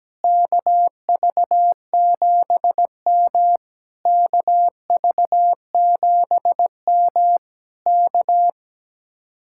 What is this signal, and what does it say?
Morse code, heard as a keyed tone, about 17 wpm, 700 Hz. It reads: KV7M KV7M K
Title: Speed-Racing Common QSO Phrases Encore - 17wpm